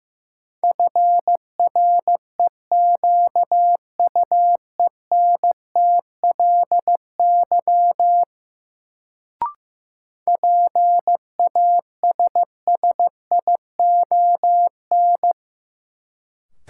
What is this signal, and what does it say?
Morse code, heard as a keyed tone, about 15 wpm, 700 Hz. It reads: FREQUENTLY PASSION
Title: Sentences from Top 1300 Words - New Word Drill - 15wpm